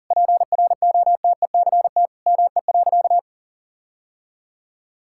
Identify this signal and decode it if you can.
Morse code, heard as a keyed tone, about 40 wpm, 700 Hz. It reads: PROTECT ME.
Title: Sentences from Top 1600 Words 40wpm